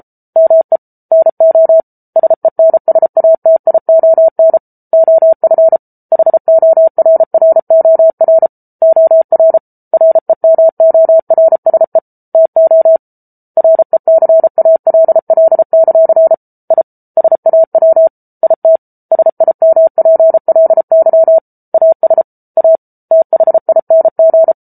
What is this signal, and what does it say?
Morse code, heard as a keyed tone, about 34 wpm, 650 Hz. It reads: ME NO SENSATION OF HORROR OR REMORSE TO RECALL; I SAW IT SIMPLY AS A THING